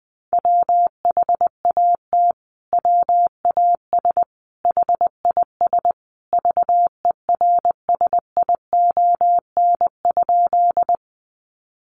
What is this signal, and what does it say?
Morse code, heard as a keyed tone, about 20 wpm, 700 Hz. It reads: WHAT WAS HIS VERSION?
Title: Sentences from Top 1400 Words 20wpm